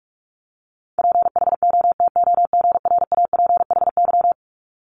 Morse code, heard as a keyed tone, about 46 wpm, 700 Hz. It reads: PHOTOGRAPHY
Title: Multi-letter Phonemes 45wpm